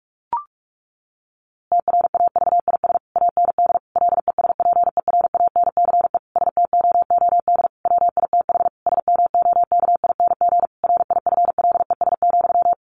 Morse code, heard as a keyed tone, about 44 wpm, 700 Hz. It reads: TRAVIS AND LESPERANCE STOOD WITH SMOKING RIFLES,
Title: A Sound Of Thunder 45wpm